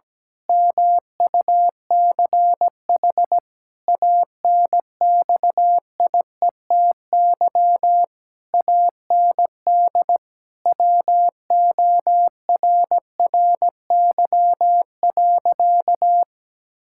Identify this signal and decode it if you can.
Morse code, heard as a keyed tone, about 17 wpm, 700 Hz. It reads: MUCH ANXIETY AND WORRY.
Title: Pangram Challenge v2 - 17wpm